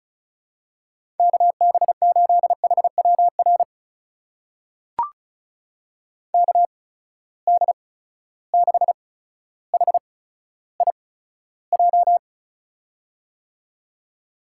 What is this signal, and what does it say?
Morse code, heard as a keyed tone, about 36 wpm, spaced as for 15 wpm, 700 Hz. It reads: KB8HWR KD6HIJ
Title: USA Call Signs - 6 Characters - 8x Character Spacing - 35wpm